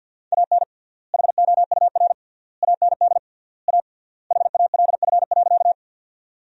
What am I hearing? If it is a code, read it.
Morse code, heard as a keyed tone, about 50 wpm, 700 Hz. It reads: AN HOUR AND A HALF.
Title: 3x Word-Spacing - Sentences from Top 400 Words - 50wpm